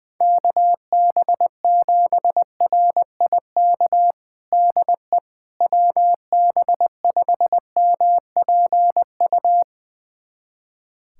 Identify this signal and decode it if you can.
Morse code, heard as a keyed tone, about 20 wpm, 700 Hz. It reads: KB7RIK DE WB5MPU
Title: USA Callsign DE Callsign - 6 Characters - 20wpm